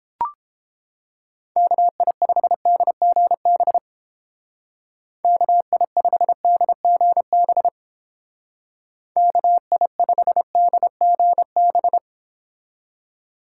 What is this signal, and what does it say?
Morse code, heard as a keyed tone, about 29 wpm, 700 Hz. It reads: KI5DGB KI5DGB KI5DGB